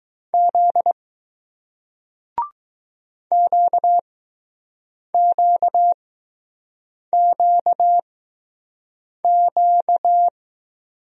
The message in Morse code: Z QQQQ

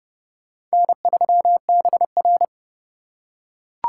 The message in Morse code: N3BR

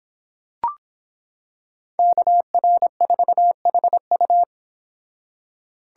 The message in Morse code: KR4HU

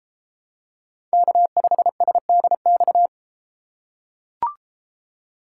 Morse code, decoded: K5SDX

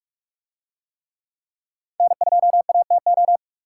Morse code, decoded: N2ATO

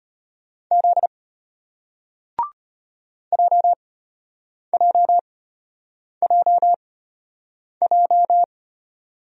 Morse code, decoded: Z JJJJ

